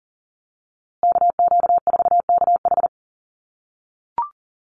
Morse code, heard as a keyed tone, about 40 wpm, 700 Hz. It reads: KQ4KH
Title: USA Call Signs Repeated 2x 40wpm